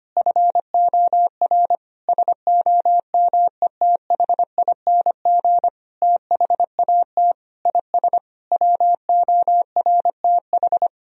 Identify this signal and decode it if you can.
Morse code, heard as a keyed tone, about 25 wpm, 700 Hz. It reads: FOR SOMETHING THAT IS WORTH
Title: Sentences from Top 1000 Words - Review Part 1 - 25wpm